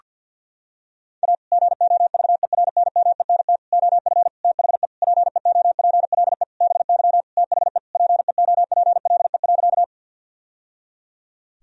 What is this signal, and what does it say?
Morse code, heard as a keyed tone, about 52 wpm, 700 Hz. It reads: A GOVERNMENT OF THE PEOPLE BY THE PEOPLE.
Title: Sentences from Top 1000 Words - Review Part 3 - 50wpm